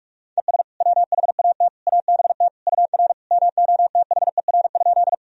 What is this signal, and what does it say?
Morse code, heard as a keyed tone, about 46 wpm, 700 Hz. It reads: ES WHAT ABT UR MOTHER?
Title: Sentences from Top 200 Words and Abbreviations 45wpm